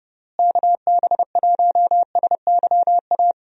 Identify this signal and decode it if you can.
Morse code, heard as a keyed tone, about 31 wpm, 700 Hz. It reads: KB1SYA
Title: USA Call Signs - 6 Characters - Repeated 2x - 30wpm